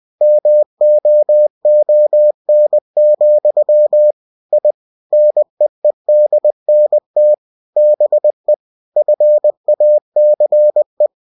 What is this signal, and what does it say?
Morse code, heard as a keyed tone, about 20 wpm, 600 Hz. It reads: MOON, I NEEDNT BE FACE
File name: cw-books / winnie-the-pooh / winnie-20wpm-page-0053